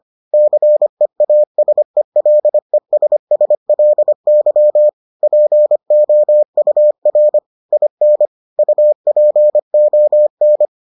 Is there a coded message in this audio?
Yes: Morse code from 0.3 to 10.7 s, heard as a keyed tone, about 25 wpm, 600 Hz. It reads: CEASELESSLY POUR IN UPON